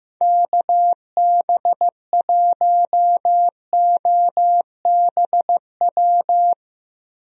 Morse code, heard as a keyed tone, about 15 wpm, 700 Hz. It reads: KB1OBW
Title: USA Call Signs - 6 Characters - Repeated 3x - 15wpm